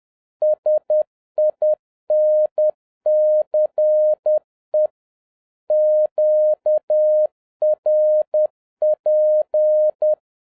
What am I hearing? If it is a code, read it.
Morse code, heard as a keyed tone, about 10 wpm, 610 Hz. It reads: SINCE QRP
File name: 2011-07-mp3 / n41 EXCITING QRP CW OPERATION 10wpm